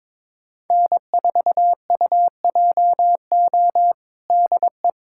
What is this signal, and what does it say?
Morse code, heard as a keyed tone, about 21 wpm, 700 Hz. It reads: N4UJO DE
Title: USA Callsign DE Callsign - 5 Characters - 22wpm